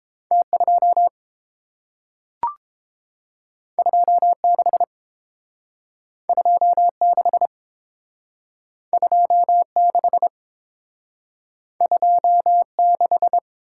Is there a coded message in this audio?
Yes: Morse code from 0.3 to 13.4 s, heard as a keyed tone, about 30 wpm, 700 Hz. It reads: T2 26 26 26 26